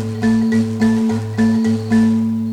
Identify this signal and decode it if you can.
There is no code.